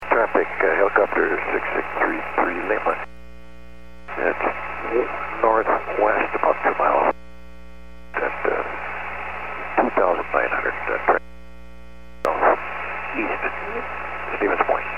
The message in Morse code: none

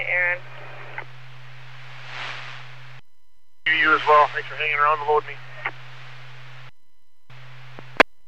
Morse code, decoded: none

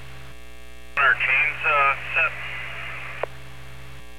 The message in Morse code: none

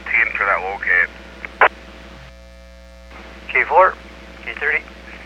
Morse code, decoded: none

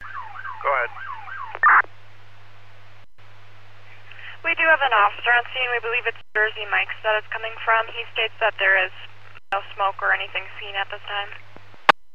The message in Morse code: none

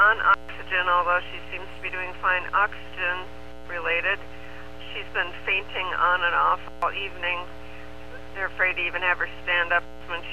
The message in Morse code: none

